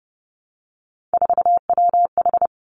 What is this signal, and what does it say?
Morse code, heard as a keyed tone, about 31 wpm, 700 Hz. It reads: 4WH
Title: Three Challenging Letter-Number Combinations 30wpm